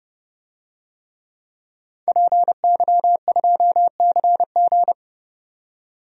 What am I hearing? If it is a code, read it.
Morse code, heard as a keyed tone, about 30 wpm, 700 Hz. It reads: PY2CG
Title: DXCC-100 Call Signs - 5 Characters - 30wpm